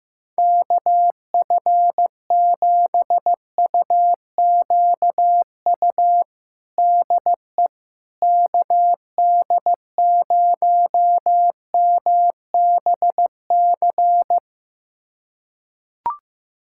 Morse code, heard as a keyed tone, about 15 wpm, 700 Hz. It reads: KF7UQU DE KD0MBC
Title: USA Callsign DE Callsign - 6 Characters - 15wpm